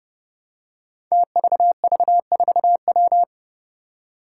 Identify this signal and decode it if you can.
Morse code, heard as a keyed tone, about 30 wpm, 700 Hz. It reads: TVV4W
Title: Five Challenging Letter-Number Combinations 30wpm